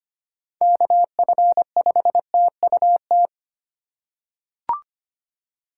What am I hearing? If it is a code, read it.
Morse code, heard as a keyed tone, about 25 wpm, 700 Hz. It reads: KF5TUT
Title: USA Call Signs - 6 Characters - 3x Character Spacing - 25wpm